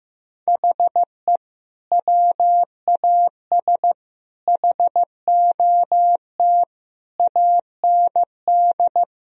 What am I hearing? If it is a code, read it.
Morse code, heard as a keyed tone, about 15 wpm, 700 Hz. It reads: HE WAS HOT AND
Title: Sentences from Top 1000 Words - Review Part 3 - 15wpm